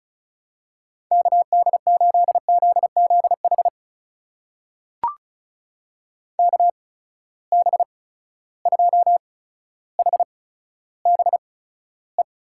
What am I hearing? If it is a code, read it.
Morse code, heard as a keyed tone, about 36 wpm, spaced as for 15 wpm, 700 Hz. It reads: KD8ZZH KB2HBE
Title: USA Call Signs - 6 Characters - 8x Character Spacing - 35wpm